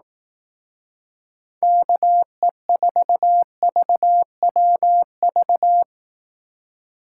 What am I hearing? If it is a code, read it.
Morse code, heard as a keyed tone, about 18 wpm, 700 Hz. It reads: KE4VWV
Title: Speed Racing - USA Call Signs 15wpm